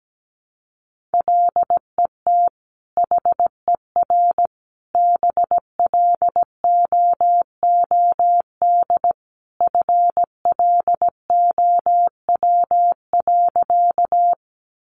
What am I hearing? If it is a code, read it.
Morse code, heard as a keyed tone, about 17 wpm, 700 Hz. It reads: LET HER BLOOD FLOW.
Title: Sentences from Top 1200 Words 17wpm